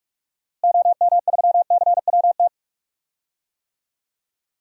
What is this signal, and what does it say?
Morse code, heard as a keyed tone, about 46 wpm, 700 Hz. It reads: OM3CWT
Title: DXCC-100 Call Signs - 6 Characters - Repeated 2x - 45wpm